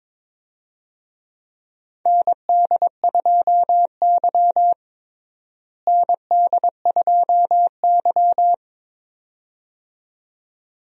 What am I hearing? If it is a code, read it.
Morse code, heard as a keyed tone, about 22 wpm, 700 Hz. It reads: ND2Y ND2Y
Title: USA Call Signs - 4 Characters - Repeated 2x - 22wpm